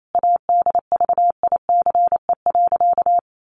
Morse code, heard as a keyed tone, about 28 wpm, 700 Hz. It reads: ADVICE.
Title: Sentences from Top 1100 Words 28wpm